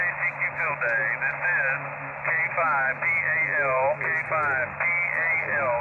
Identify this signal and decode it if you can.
Emergency Alert System (SAME) tones (header bursts), heard as harsh buzzing bursts, with speech. It is heard from 0.8 to 5.5 s.